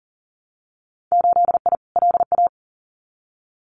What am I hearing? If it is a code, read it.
Morse code, heard as a keyed tone, about 40 wpm, 700 Hz. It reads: 8I LA
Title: Winter Field Day Exchanges Encore 40wpm